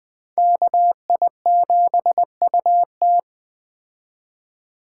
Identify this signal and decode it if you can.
Morse code, heard as a keyed tone, about 20 wpm, 700 Hz. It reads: KI7UT